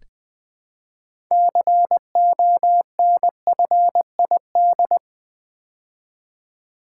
Morse code, heard as a keyed tone, about 20 wpm, 700 Hz. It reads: CONFID
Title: Build A Word - 3 Letters At A Time - 9-Letter Words - 20wpm